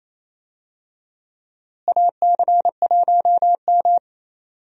Morse code, heard as a keyed tone, about 28 wpm, 700 Hz. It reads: AC1M
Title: USA Call Signs - 4 Characters - Repeated 3x - 28wpm